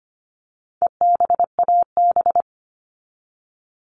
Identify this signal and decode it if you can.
Morse code, heard as a keyed tone, about 25 wpm, 700 Hz. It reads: EBAB
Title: All Cut Numbers - 4-Digit 25wpm